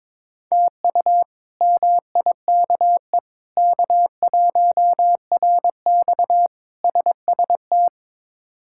Morse code, heard as a keyed tone, about 22 wpm, 700 Hz. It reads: TU MIKE K1RX SST
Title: K1USN SST QSO Practice - 22wpm